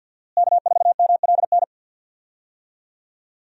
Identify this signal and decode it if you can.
Morse code, heard as a keyed tone, about 51 wpm, 700 Hz. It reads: K4MLN
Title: USA Call Signs - 5 Characters - Repeated 3x - 50wpm